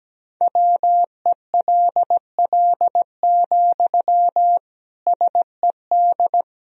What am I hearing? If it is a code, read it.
Morse code, heard as a keyed tone, about 17 wpm, 700 Hz. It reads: WELL, SED
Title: Sentences from Top 400 Words and Abbreviations 17wpm